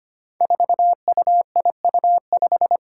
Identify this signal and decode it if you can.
Morse code, heard as a keyed tone, about 25 wpm, 700 Hz. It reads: 4UIU5
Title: Five Challenging Letter-Number Combinations 25wpm